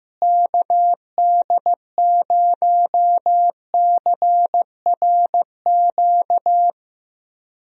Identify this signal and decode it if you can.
Morse code, heard as a keyed tone, about 15 wpm, 700 Hz. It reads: KD0CRQ
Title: USA Call Signs - 6 Characters - Repeated 3x - 15wpm